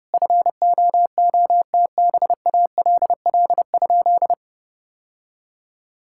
Morse code, heard as a keyed tone, about 30 wpm, 700 Hz. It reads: FOOTBALL?
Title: Sentences from Top 1600 Words 30wpm